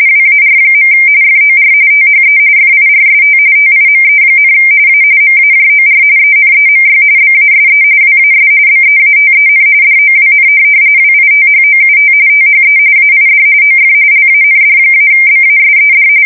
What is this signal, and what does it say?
RTTY, about 45.45 baud. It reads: WAR. FOR THE FIRST TIME, A MAJOR ENEMY BASE WAS BEATEN DOWN WITHOUT THE AID OF LAND-BASED
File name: fsk170 / USS-New-Jersey